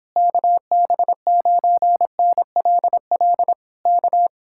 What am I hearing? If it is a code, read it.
Morse code, heard as a keyed tone, about 26 wpm, 700 Hz. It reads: KB9NLL K